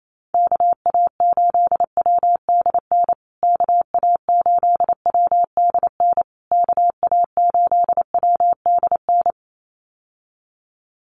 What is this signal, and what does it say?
Morse code, heard as a keyed tone, about 28 wpm, 700 Hz. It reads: KA8WDN KA8WDN KA8WDN
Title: USA Call Signs Repeated 3x 28wpm